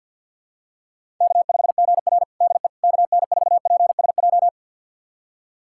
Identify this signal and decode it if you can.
Morse code, heard as a keyed tone, about 50 wpm, 700 Hz. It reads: K5GR DE KN4PSJ
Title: USA Callsign DE Callsign 50wpm